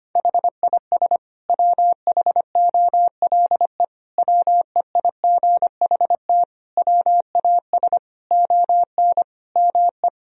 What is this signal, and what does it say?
Morse code, heard as a keyed tone, about 25 wpm, 700 Hz. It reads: HIS WHOLE WEIGHT WAS ON ME
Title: Sentences from Top 1000 Words - Review Part 3 - 25wpm